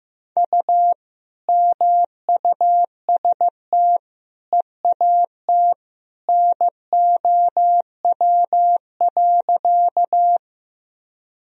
Morse code, heard as a keyed tone, about 15 wpm, 700 Hz. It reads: U MUST EAT NOW.
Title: Sentences from Top 300 Words and Abbreviations 15wpm